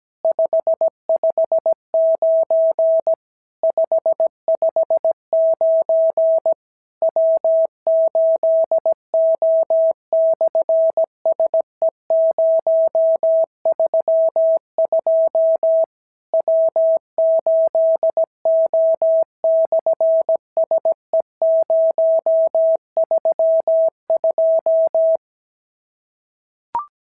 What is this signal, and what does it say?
Morse code, heard as a keyed tone, about 17 wpm, 690 Hz. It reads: 559 559 W8O/SE032 W8O/SE032
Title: SOTA QSO Practice 17wpm